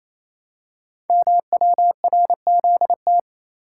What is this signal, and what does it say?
Morse code, heard as a keyed tone, about 29 wpm, 700 Hz. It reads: MWRZT
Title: Code Groups - 5 Letters - 28wpm